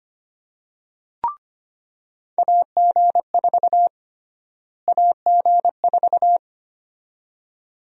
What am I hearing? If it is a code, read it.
Morse code, heard as a keyed tone, about 25 wpm, 700 Hz. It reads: AG4 AG4
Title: USA Call Sign Prefixes - Repeated 2x - 25wpm